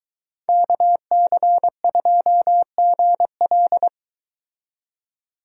Morse code, heard as a keyed tone, about 23 wpm, 700 Hz. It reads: KC2GL